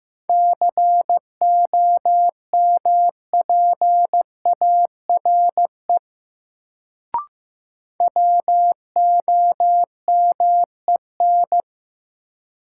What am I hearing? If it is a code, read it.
Morse code, heard as a keyed tone, about 15 wpm, 700 Hz. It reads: COMPARE WOMEN
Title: Sentences from Top 1300 Words - New Word Drill - 15wpm